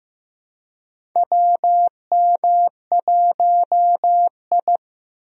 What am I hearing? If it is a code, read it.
Morse code, heard as a keyed tone, about 16 wpm, 700 Hz. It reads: WM1I
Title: USA Call Signs - 4 Characters - Repeated 3x - 15wpm